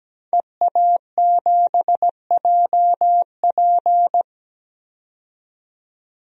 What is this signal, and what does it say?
Morse code, heard as a keyed tone, about 17 wpm, 700 Hz. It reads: EA7JP
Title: DXCC-100 Call Signs - 5 Characters - 17wpm